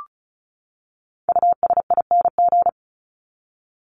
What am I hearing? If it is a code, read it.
Morse code, heard as a keyed tone, about 34 wpm, 700 Hz. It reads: USING
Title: ING Suffix - Top 100 Words with Base Words 35wpm